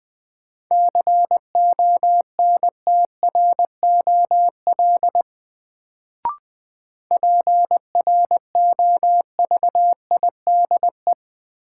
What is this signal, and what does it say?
Morse code, heard as a keyed tone, about 21 wpm, 700 Hz. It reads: CONTROL PROVIDE
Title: Sentences from Top 1000 Words - Review Part 2 - New Word Drill - 20wpm